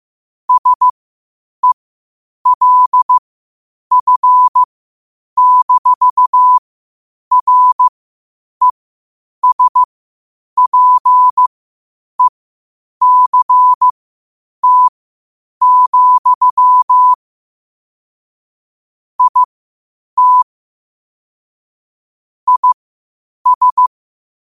Morse code, heard as a keyed tone, about 15 wpm, spaced as for 8 wpm, 1000 Hz. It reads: SELF-RESPECT, IT IS
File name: qotdfiles / 8wpm